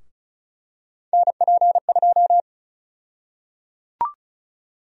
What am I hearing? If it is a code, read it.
Morse code, heard as a keyed tone, about 36 wpm, 700 Hz. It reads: NP2
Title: USA Call Sign Prefixes - 35wpm